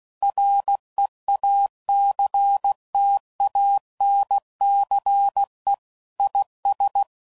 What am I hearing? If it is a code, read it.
Morse code, heard as a keyed tone, about 16 wpm, 800 Hz. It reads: REACTANCE IS
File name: CW Code Groups @15WPM/Technical Subjects